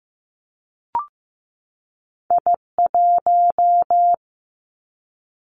Morse code, heard as a keyed tone, about 15 wpm, 700 Hz. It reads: I1
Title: Two Challenging Letter-Number Combinations 15wpm